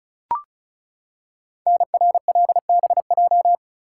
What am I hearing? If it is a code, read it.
Morse code, heard as a keyed tone, about 33 wpm, 700 Hz. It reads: NRLBJ
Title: Code Groups - 5 Letters - 35wpm